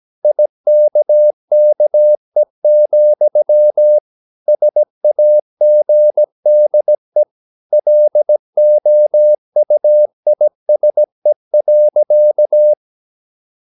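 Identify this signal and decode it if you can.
Morse code, heard as a keyed tone, about 17 wpm, 600 Hz. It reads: IKKE, SAGDE LOUISE.
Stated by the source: Slow (QRS) Morse practice at 17 wpm, from CW med Gnister.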